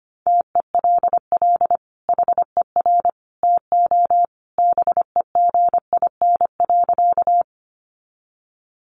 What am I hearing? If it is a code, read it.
Morse code, heard as a keyed tone, about 25 wpm, 700 Hz. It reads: TELL HER TO BEGIN.
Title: Sentences from Top 300 Words 25wpm